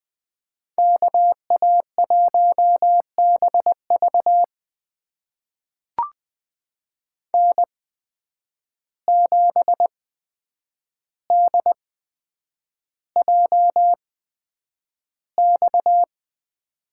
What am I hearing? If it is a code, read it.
Morse code, heard as a keyed tone, about 20 wpm, spaced as for 9 wpm, 700 Hz. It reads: KA1BV N7DJX
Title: USA Call Signs - 5 Characters - 8x Character Spacing - 20wpm